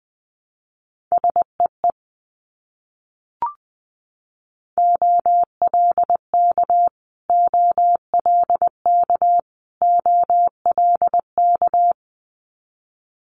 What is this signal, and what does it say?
Morse code, heard as a keyed tone, about 20 wpm, 700 Hz. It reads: SEE OLK OLK OLK
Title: Multi-letter Phonemes Repeated 3x 20wpm